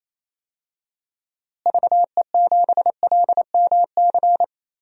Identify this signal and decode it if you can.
Morse code, heard as a keyed tone, about 27 wpm, 700 Hz. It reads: VE7LMC
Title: DXCC-100 Call Signs - 6 Characters - Repeated 3x - 28wpm